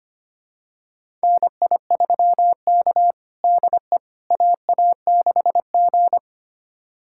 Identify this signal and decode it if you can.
Morse code, heard as a keyed tone, about 25 wpm, 700 Hz. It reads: NI3K DE AA6G
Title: USA Callsign DE Callsign - 4 Characters - 25wpm